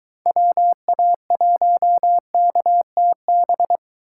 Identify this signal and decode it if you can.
Morse code, heard as a keyed tone, about 23 wpm, 700 Hz. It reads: WA1KTB